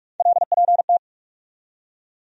Morse code, heard as a keyed tone, about 46 wpm, 700 Hz. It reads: LPT
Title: USA Call Sign Postfixes - 3 Letters 45wpm